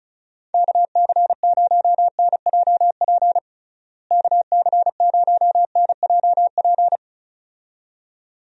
Morse code, heard as a keyed tone, about 34 wpm, 700 Hz. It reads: KC0NJP KC0NJP
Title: USA Call Signs - 6 Characters - Repeated 2x - 35wpm